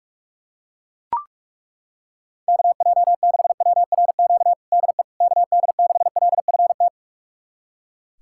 Morse code, heard as a keyed tone, about 41 wpm, 700 Hz. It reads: KJ6WRQ DE KD6LFT
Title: USA Callsign DE Callsign - 6 Characters - 45wpm